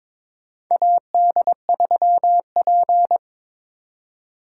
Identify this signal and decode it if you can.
Morse code, heard as a keyed tone, about 22 wpm, 700 Hz. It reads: AD3P
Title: USA Call Signs - 2x Character Spacing - 22wpm